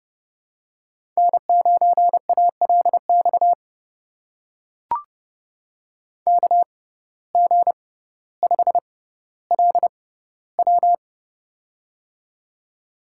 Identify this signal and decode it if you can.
Morse code, heard as a keyed tone, about 30 wpm, spaced as for 15 wpm, 700 Hz. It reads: N9ALX KG5LW
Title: USA Call Signs - 5 Characters - 6x Character Spacing - 30wpm